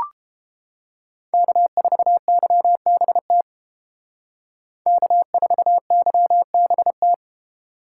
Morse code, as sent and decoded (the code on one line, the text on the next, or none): -.- ....- -.-- -... - / -.- ....- -.-- -... -
K4YBT K4YBT